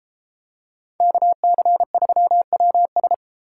Morse code, heard as a keyed tone, about 32 wpm, 700 Hz. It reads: KC3WS